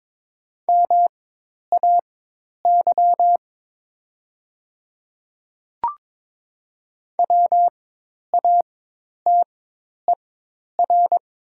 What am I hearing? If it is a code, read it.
Morse code, heard as a keyed tone, about 22 wpm, spaced as for 10 wpm, 700 Hz. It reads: MAY WATER
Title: 4x Character Spacing - Top 100 Words - 22wpm